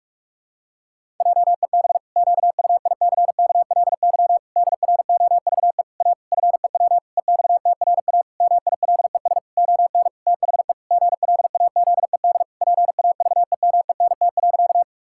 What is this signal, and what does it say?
Morse code, heard as a keyed tone, about 46 wpm, 700 Hz. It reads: JEB QUICKLY DROVE A FEW EXTRA MILES ON THE GLAZED PAVEMENT.
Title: Pangram Challenge v2 - 45wpm